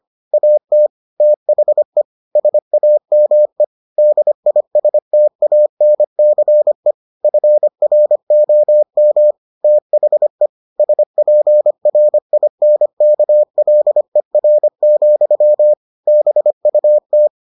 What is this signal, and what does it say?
Morse code, heard as a keyed tone, about 25 wpm, 600 Hz. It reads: AT THE SAME DISTANCE FROM THE SPRINKLER, BUT